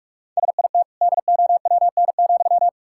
Morse code, heard as a keyed tone, about 46 wpm, 700 Hz. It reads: SIT DOWN,
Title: Sentences from Top 1000 Words - Review Part 6 - 45wpm